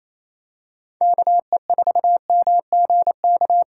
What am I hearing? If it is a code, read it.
Morse code, heard as a keyed tone, about 28 wpm, 700 Hz. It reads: KE4MGK
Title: USA Call Signs - 6 Characters - 3x Character Spacing - 28wpm